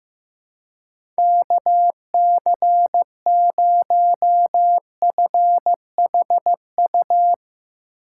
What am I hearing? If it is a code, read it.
Morse code, heard as a keyed tone, about 15 wpm, 700 Hz. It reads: KC0FHU
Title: USA Call Signs - 6 Characters - 6x Character Spacing - 15wpm